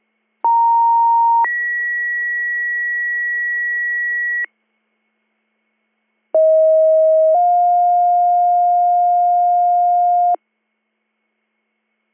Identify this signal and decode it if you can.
Two-tone page, heard as a one-second tone, then a three-second tone, about 930 then 1950 Hz, about 640 then 710 Hz. It is heard from 0.4 to 10.4 s.